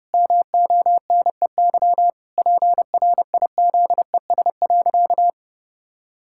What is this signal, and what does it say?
Morse code, heard as a keyed tone, about 32 wpm, 700 Hz. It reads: MONEY PRIZES.
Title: Pangram Challenge v2 - 30wpm